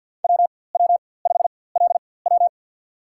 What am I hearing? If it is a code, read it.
Morse code, heard as a keyed tone, about 52 wpm, spaced as for 24 wpm, 700 Hz. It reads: WW5LW
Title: USA Call Signs - 5 Characters - 4x Character Spacing - 50wpm